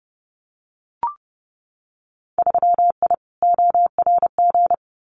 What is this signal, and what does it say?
Morse code, heard as a keyed tone, about 30 wpm, 700 Hz. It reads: 3I ORG
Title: Winter Field Day Exchanges Encore 30wpm